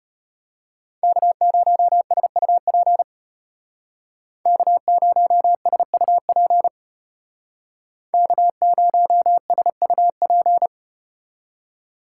Morse code, heard as a keyed tone, about 35 wpm, 700 Hz. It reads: K0SUP K0SUP K0SUP